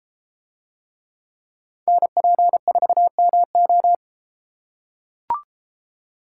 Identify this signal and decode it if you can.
Morse code, heard as a keyed tone, about 32 wpm, 700 Hz. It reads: NP4MO